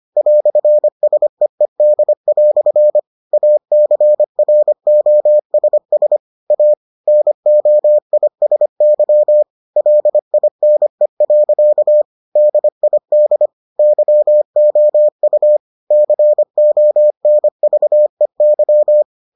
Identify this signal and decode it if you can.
Morse code, heard as a keyed tone, about 25 wpm, 600 Hz. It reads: "SEED" ACROSS A NOISY LINE. DID YOU CONVEY